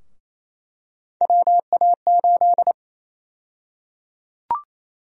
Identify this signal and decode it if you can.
Morse code, heard as a keyed tone, about 28 wpm, 700 Hz. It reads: WA8
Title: USA Call Sign Prefixes - Repeated 2x - 28wpm